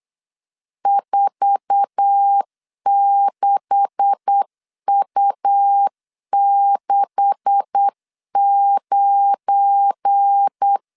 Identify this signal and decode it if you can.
Morse code, heard as a keyed tone, about 8 wpm, 800 Hz. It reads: 46U69
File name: LEZIONI / 9 286 8wpm 3425